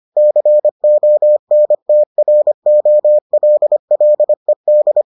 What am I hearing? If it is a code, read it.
Morse code, heard as a keyed tone, about 25 wpm, 600 Hz. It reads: CONTROLLED